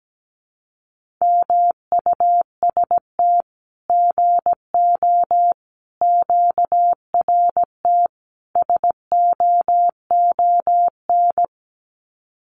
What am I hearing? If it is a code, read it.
Morse code, heard as a keyed tone, about 17 wpm, 700 Hz. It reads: MUST GO QRT SOON
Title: Common QSO Phrases 17wpm